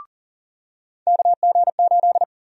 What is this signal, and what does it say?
Morse code, heard as a keyed tone, about 39 wpm, 700 Hz. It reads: KG8
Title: USA Call Sign Prefixes - 40wpm